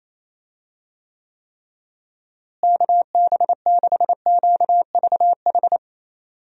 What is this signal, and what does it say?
Morse code, heard as a keyed tone, about 28 wpm, 700 Hz. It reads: KB6QVH
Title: USA Call Signs - 6 Characters - Repeated 3x - 28wpm